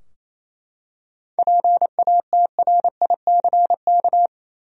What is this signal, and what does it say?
Morse code, heard as a keyed tone, about 28 wpm, 700 Hz. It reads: PATRICK
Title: USA Common Names - 28wpm